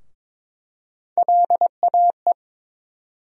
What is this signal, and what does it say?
Morse code, heard as a keyed tone, about 22 wpm, 700 Hz. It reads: LAE
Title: USA Call Sign Postfixes - 3 Letters 22wpm